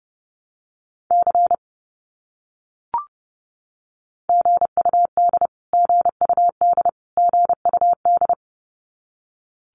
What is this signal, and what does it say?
Morse code, heard as a keyed tone, about 30 wpm, 700 Hz. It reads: C GUD GUD GUD
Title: Abbreviations Repeated 3x - 30wpm